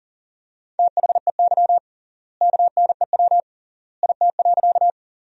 Morse code, heard as a keyed tone, about 40 wpm, 700 Hz. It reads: THEY KNEW IT.
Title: 3x Word-Spacing - Sentences from Top 400 Words - 40wpm